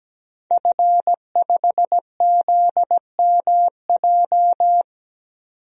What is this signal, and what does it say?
Morse code, heard as a keyed tone, about 17 wpm, 700 Hz. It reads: F5ZMJ
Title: DXCC-100 Call Signs - 5 Characters - 17wpm